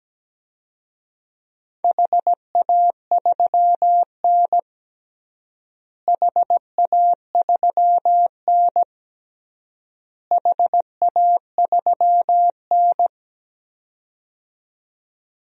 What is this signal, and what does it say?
Morse code, heard as a keyed tone, about 17 wpm, 700 Hz. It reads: HA3N HA3N HA3N
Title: DXCC-100 Call Signs - 4 Characters - Repeated 3x - 17wpm